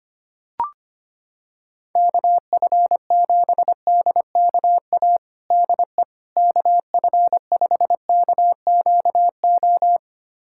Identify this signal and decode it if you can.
Morse code, heard as a keyed tone, about 25 wpm, 700 Hz. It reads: KF7DKA DE KF5KQO
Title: USA Callsign DE Callsign - 6 Characters - 25wpm